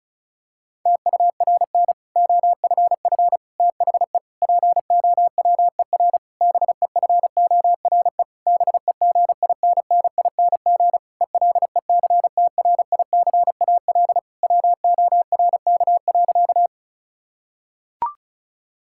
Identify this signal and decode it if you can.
Morse code, heard as a keyed tone, about 34 wpm, 700 Hz. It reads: TURN OFF THE POWER BEFORE BEGINNING ELECTRICAL WORK.
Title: Sentences from Top 1000 Words - Review Part 6 - 35wpm